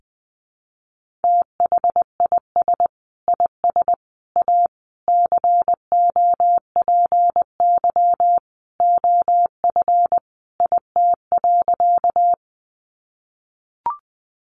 Morse code, heard as a keyed tone, about 20 wpm, 700 Hz. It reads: THIS IS A COPY OF IT.
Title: Sentences from Top 1400 Words 20wpm